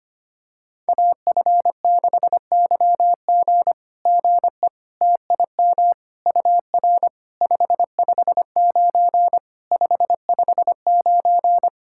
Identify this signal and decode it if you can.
Morse code, heard as a keyed tone, about 25 wpm, 700 Hz. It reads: AF6YG GE TIM UR 559 559
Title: SOTA QSO Practice 25wpm